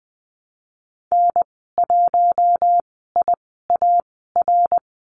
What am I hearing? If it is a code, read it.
Morse code, heard as a keyed tone, about 20 wpm, spaced as for 15 wpm, 700 Hz. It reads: N1IAR
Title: 055 Lesson 10g - Letter r - Call Signs 20wpm